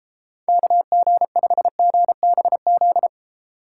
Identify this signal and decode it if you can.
Morse code, heard as a keyed tone, about 32 wpm, 700 Hz. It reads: KG5GBZ